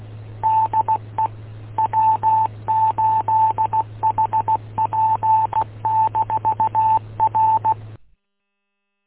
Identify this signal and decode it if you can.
Morse code, heard as a keyed tone, about 16 wpm, 880 Hz. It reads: DE W8HP-R
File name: w8um- -combined